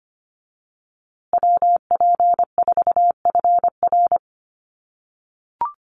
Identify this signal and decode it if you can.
Morse code, heard as a keyed tone, about 25 wpm, 700 Hz. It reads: WP4FR
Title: USA Call Signs Repeated 2x 25wpm